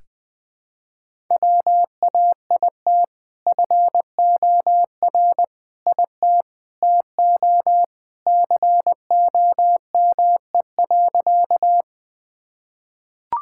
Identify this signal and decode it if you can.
Morse code, heard as a keyed tone, about 20 wpm, 700 Hz. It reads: WAIT FOR IT TO COME.
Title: Sentences with 2-Word Phrases - Spoken Beginning - 20wpm